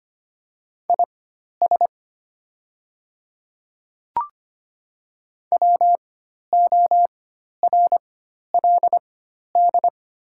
Morse code, heard as a keyed tone, about 25 wpm, spaced as for 12 wpm, 700 Hz. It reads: IS WORLD
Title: 4x Character Spacing - Top 200 Words - 25wpm